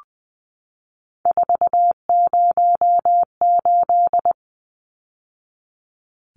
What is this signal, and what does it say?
Morse code, heard as a keyed tone, about 19 wpm, 700 Hz. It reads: 408
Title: Numbers - 3-Digit 20wpm